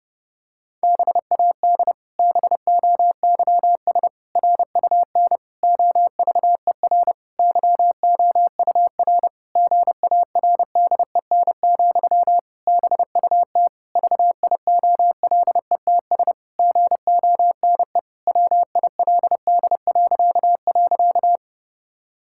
Morse code, heard as a keyed tone, about 30 wpm, 700 Hz. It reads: BAD BOYS RUN OVER YOUR GARDEN, BUT VIOLETS GONE WILD..
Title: Pangram Challenge v2 - 30wpm